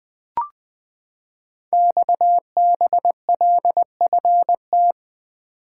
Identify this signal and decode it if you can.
Morse code, heard as a keyed tone, about 20 wpm, 700 Hz. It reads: XBLFT
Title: Code Groups - 5 Letters - 20wpm